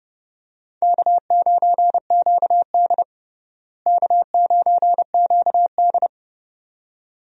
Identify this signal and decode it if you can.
Morse code, heard as a keyed tone, about 30 wpm, 700 Hz. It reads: K9QD K9QD
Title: USA Call Signs - 4 Characters - Repeated 2x - 30wpm